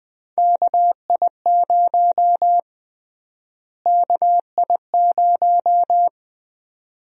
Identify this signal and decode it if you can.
Morse code, heard as a keyed tone, about 21 wpm, 700 Hz. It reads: KI0 KI0
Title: USA Call Sign Prefixes - Repeated 2x - 20wpm